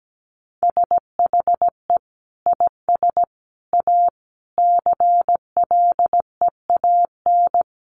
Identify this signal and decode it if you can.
Morse code, heard as a keyed tone, about 17 wpm, 700 Hz. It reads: SHE IS A CLEAN
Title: Sentences from Top 1100 Words 17wpm